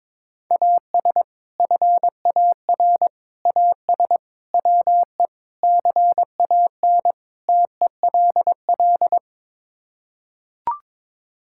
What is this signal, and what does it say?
Morse code, heard as a keyed tone, about 22 wpm, 700 Hz. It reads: AS FAR AS WE CAN TELL
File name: practice / 22wpm / Sets of 6 Words - 22wpm